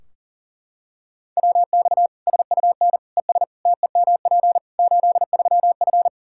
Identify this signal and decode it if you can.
Morse code, heard as a keyed tone, about 40 wpm, 700 Hz. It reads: WX SUN ES TEMP 83F
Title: K1USN SST QSO Practice 40wpm